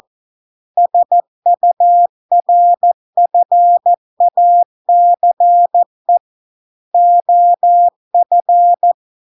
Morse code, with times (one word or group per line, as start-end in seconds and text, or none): 0.8-6.2 SURFACE
6.9-8.9 OF